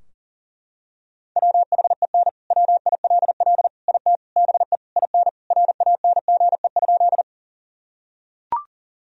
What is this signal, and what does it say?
Morse code, heard as a keyed tone, about 40 wpm, 700 Hz. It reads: WHEN WILL IT BE IN RANGE?
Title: Sentences from Top 1000 Words - Review Part 4 - 40wpm